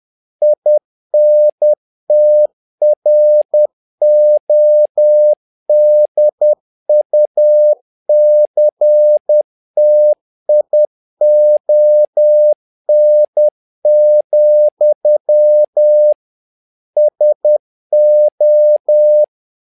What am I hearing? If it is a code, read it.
Morse code, heard as a keyed tone, about 10 wpm, 600 Hz. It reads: INTRODUCTION, SO
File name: learning center / tower trouble / tower trouble 10wpm